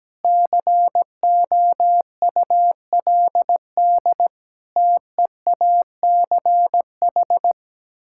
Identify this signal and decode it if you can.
Morse code, heard as a keyed tone, about 17 wpm, 700 Hz. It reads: COULD TEACH
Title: Sentences from Top 1000 Words - Review Part 6 - 17wpm